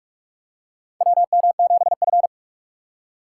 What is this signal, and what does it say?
Morse code, heard as a keyed tone, about 42 wpm, 700 Hz. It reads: WM7F
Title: USA Call Signs - 4 Characters - 3x Character Spacing - 45wpm